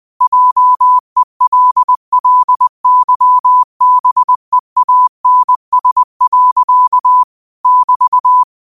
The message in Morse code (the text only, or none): JELLYBEANS. <BT>